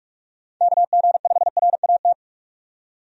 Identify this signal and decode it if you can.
Morse code, heard as a keyed tone, about 46 wpm, 700 Hz. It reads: KG5RAT
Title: USA Call Signs - 6 Characters - 45wpm